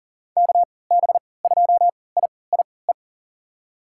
Morse code, heard as a keyed tone, about 40 wpm, spaced as for 23 wpm, 700 Hz. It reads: KB2IIE
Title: USA Call Signs - 6 Characters - 3x Character Spacing - 40wpm